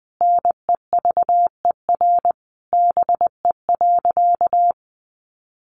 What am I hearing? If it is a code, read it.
Morse code, heard as a keyed tone, about 20 wpm, 700 Hz. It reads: NEVER BE.
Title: Sentences from Top 300 Words 20wpm